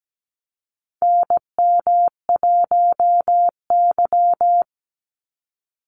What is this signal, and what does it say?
Morse code, heard as a keyed tone, about 17 wpm, 700 Hz. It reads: NM1Y
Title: Call Signs - 17wpm